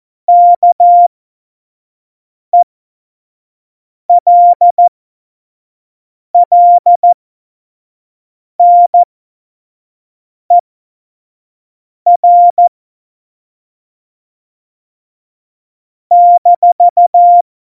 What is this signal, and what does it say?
Morse code, heard as a keyed tone, about 14 wpm, spaced as for 5 wpm, 700 Hz. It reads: KELLNER -
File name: Morsekurs / Witze / Witz 12 Mai 2020, 25 F70